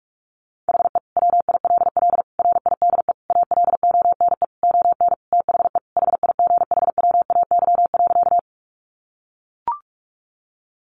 Morse code, heard as a keyed tone, about 44 wpm, 700 Hz. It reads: HE WILL RIDE ALONE ON THE HIGHWAY.
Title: Sentences from Top 1400 Words 45wpm